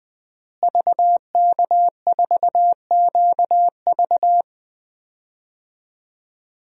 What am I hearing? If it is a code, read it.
Morse code, heard as a keyed tone, about 20 wpm, 700 Hz. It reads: VK4QV
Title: DXCC-100 Call Signs - 5 Characters - Repeated 3x - 20wpm